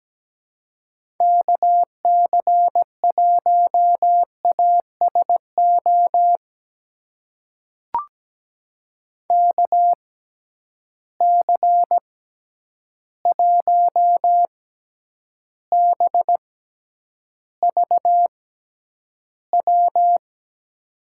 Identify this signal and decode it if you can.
Morse code, heard as a keyed tone, about 17 wpm, spaced as for 9 wpm, 700 Hz. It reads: KC1ASO KC1BVW
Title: USA Call Signs - 6 Characters - 6x Character Spacing - 17wpm